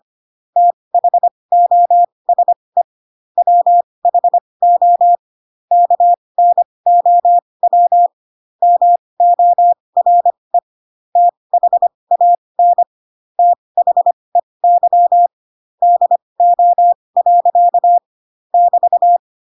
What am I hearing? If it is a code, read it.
Morse code, heard as a keyed tone, about 25 wpm, spaced as for 20 wpm, 700 Hz. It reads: THOSE WHO KNOW MORE THAN THEY DO. <BT>
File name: Pinocchio / Pinocchio ch4 15 25 ik1hge